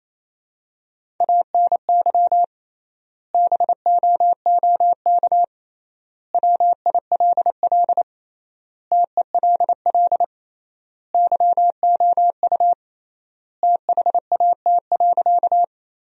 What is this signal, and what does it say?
Morse code, heard as a keyed tone, about 28 wpm, 700 Hz. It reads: ANY BOOK WILL TELL YOU THAT.
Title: 3x Word-Spacing - Sentences from Top 300 Words - 28wpm